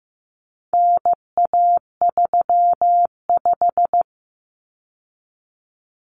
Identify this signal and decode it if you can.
Morse code, heard as a keyed tone, about 15 wpm, 700 Hz. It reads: NA35
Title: Common Cut Numbers - 4-Digit 15wpm